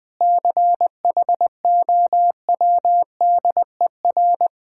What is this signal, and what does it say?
Morse code, heard as a keyed tone, about 20 wpm, 690 Hz. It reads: CHOWDER
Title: Difficult Words with Hints - 20wpm